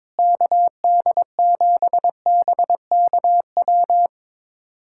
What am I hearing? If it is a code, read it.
Morse code, heard as a keyed tone, about 22 wpm, 700 Hz. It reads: KD7BKW
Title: USA Callsign DE Callsign - 6 Characters - 22wpm